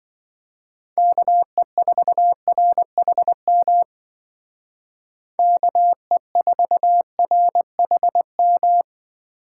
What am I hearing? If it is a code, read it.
Morse code, heard as a keyed tone, about 24 wpm, 700 Hz. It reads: KE4RHM KE4RHM
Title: Speed Racing - USA Call Signs 20wpm